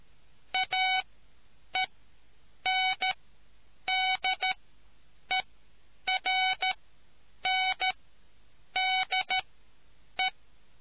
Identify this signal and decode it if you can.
Morse code, heard as a keyed tone, about 13 wpm, spaced as for 8 wpm, 750 Hz. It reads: AENDERNDE